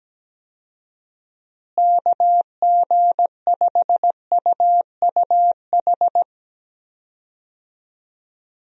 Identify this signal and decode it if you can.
Morse code, heard as a keyed tone, about 17 wpm, 700 Hz. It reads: KG5UUH
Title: USA Call Signs - 6 Characters - Repeated 3x - 17wpm